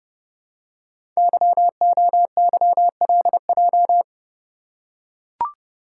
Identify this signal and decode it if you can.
Morse code, heard as a keyed tone, about 28 wpm, 700 Hz. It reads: YOYLJ
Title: Code Groups - 5 Letter:Numbers - 30wpm